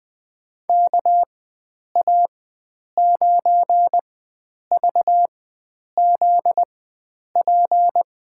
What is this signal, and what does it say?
Morse code, heard as a keyed tone, about 20 wpm, spaced as for 9 wpm, 700 Hz. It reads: KA9VZP
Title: USA Call Signs - 6 Characters - 4x Character Spacing - 20wpm